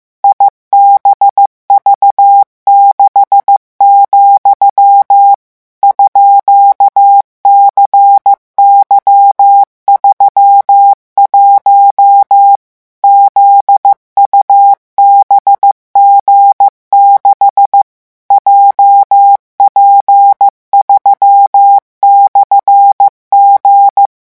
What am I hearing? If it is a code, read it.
Morse code, heard as a keyed tone, about 15 wpm, 800 Hz. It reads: IBV6, _CY31 ZUBG6 JP3/G
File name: morse mp3 / revision13-vitesse 14 mots